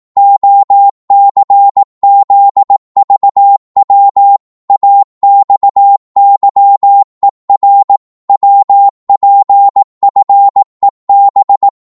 Morse code, heard as a keyed tone, about 18 wpm, 800 Hz. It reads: OCZVW AXYER WPFEB
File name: Morse / ltr-18wpm